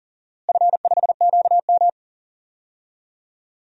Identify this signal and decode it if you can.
Morse code, heard as a keyed tone, about 40 wpm, 700 Hz. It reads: F5QM
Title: DXCC-100 Call Signs - 4 Characters - Repeated 2x - 40wpm